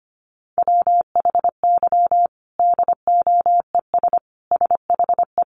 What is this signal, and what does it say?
Morse code, heard as a keyed tone, about 25 wpm, 700 Hz. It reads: WHY DOES SHE
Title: Sentences from Top 1300 Words 25wpm